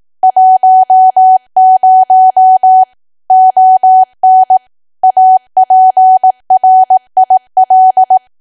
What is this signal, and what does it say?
Morse code, heard as a keyed tone, about 18 wpm, 750 Hz. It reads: 10 ON APRIL
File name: cwa / W1AW-Level2 / 18WPM